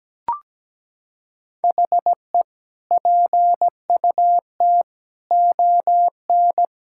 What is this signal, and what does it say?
Morse code, heard as a keyed tone, about 17 wpm, 700 Hz. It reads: HE PUT ON
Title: Sentences from Top 1600 Words 17wpm